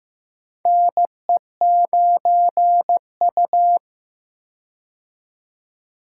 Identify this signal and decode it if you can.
Morse code, heard as a keyed tone, about 15 wpm, 700 Hz. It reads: NE9U
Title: K1USN SST QSO Practice - 15wpm